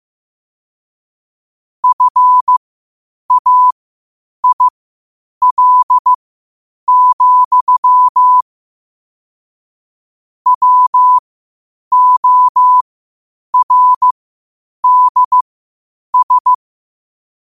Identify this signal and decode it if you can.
Morse code, heard as a keyed tone, about 15 wpm, spaced as for 8 wpm, 1000 Hz. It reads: FAIL, WORDS
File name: qotdfiles / 8wpm